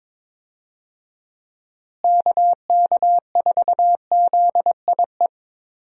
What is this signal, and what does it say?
Morse code, heard as a keyed tone, about 23 wpm, 700 Hz. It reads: KK4ZIE
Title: USA Call Signs - 6 Characters - Repeated 3x - 22wpm